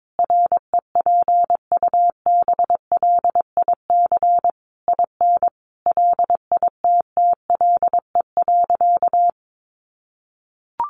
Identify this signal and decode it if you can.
Morse code, heard as a keyed tone, about 22 wpm, 700 Hz. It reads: REPUBLIC IN LITTLE.
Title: Sentences from Top 1500 Words 22wpm